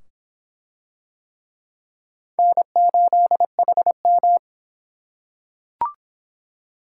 Morse code, heard as a keyed tone, about 22 wpm, 700 Hz. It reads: N8HM